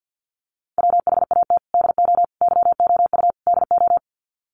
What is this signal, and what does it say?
Morse code, heard as a keyed tone, about 50 wpm, 700 Hz. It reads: WHAT DO YOU DO
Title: Sets of 4 Words 50wpm